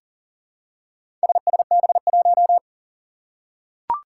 Morse code, heard as a keyed tone, about 40 wpm, 700 Hz. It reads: SSB1
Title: Four Challenging Letter-Number Combinations 40wpm